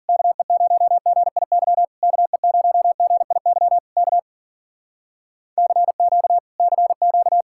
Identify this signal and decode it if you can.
Morse code, heard as a keyed tone, about 52 wpm, 700 Hz. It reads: KE0GIY KE0GIY K CQ CQ